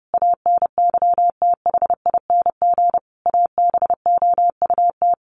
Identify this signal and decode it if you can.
Morse code, heard as a keyed tone, about 30 wpm, 700 Hz. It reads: ANYTHING ABOUT
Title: Sets of 5 Words Encore2 30wpm